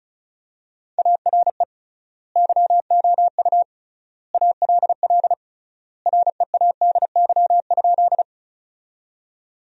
Morse code, heard as a keyed tone, about 36 wpm, 700 Hz. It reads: ARE YOU ALL READY?
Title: 3x Word-Spacing - Sentences from Top 400 Words - 35wpm